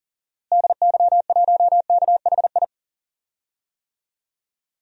Morse code, heard as a keyed tone, about 39 wpm, 700 Hz. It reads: DY1KHI
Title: DXCC-100 Call Signs - 6 Characters - Repeated 2x - 40wpm